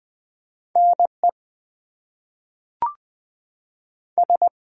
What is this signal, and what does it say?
Morse code, heard as a keyed tone, about 20 wpm, 700 Hz. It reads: NE S
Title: US States and Canadian Provinces 20wpm